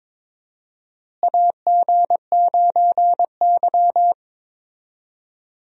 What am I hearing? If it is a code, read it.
Morse code, heard as a keyed tone, about 21 wpm, 700 Hz. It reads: AG9Y
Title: USA Call Signs - 4 Characters - Repeated 2x - 22wpm